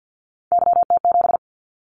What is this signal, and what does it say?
Morse code, heard as a keyed tone, about 50 wpm, 700 Hz. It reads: YT7
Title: Callsign Prefixes for Top 25 DXCC - 50wpm